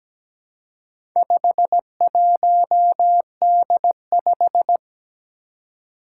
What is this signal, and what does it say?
Morse code, heard as a keyed tone, about 17 wpm, 700 Hz. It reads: 51D5
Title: Four Challenging Letter-Number Combinations 17wpm